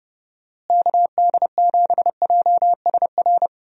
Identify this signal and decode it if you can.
Morse code, heard as a keyed tone, about 32 wpm, 700 Hz. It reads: KD7JSR
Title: USA Call Signs - 6 Characters - Repeated 3x - 30wpm